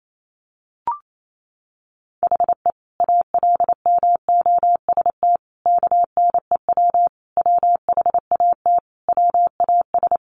Morse code, heard as a keyed tone, about 28 wpm, 700 Hz. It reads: HE ALMOST KNEW WHAT WAS
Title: Sentences from Top 700 Words 28wpm